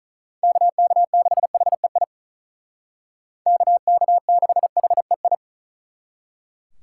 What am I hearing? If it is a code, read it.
Morse code, heard as a keyed tone, about 47 wpm, 700 Hz. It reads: KK6HEI KK6HEI